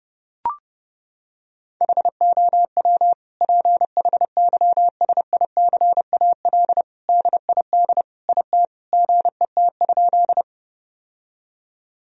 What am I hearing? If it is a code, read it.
Morse code, heard as a keyed tone, about 30 wpm, 700 Hz. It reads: HOW PHYSICAL DID IT GET?
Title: Sentences from Top 1000 Words - Review Part 4 - 30wpm